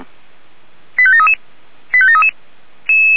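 Five-tone selective call (ZVEI), heard as quick runs of five tones, tones of about 70 ms. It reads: 85733 85733
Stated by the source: ZVEI five-tone call received with an SDR, fire brigade alerting.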